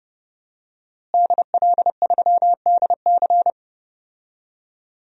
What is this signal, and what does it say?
Morse code, heard as a keyed tone, about 30 wpm, 700 Hz. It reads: DL3DC
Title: DXCC-100 Call Signs - 5 Characters - Repeated 3x - 30wpm